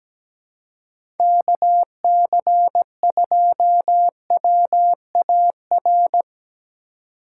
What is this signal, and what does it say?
Morse code, heard as a keyed tone, about 17 wpm, 700 Hz. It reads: KC2WAR
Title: USA Call Signs - 6 Characters - 2x Character Spacing - 17wpm